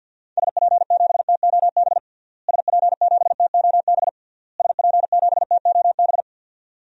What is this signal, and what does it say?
Morse code, heard as a keyed tone, about 50 wpm, 700 Hz. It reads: SP7TOB SP7TOB SP7TOB
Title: DXCC-100 Call Signs - 6 Characters - Repeated 3x - 50wpm